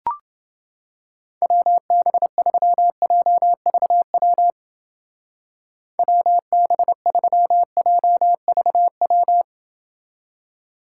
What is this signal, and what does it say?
Morse code, heard as a keyed tone, about 30 wpm, 700 Hz. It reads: WB3JVW WB3JVW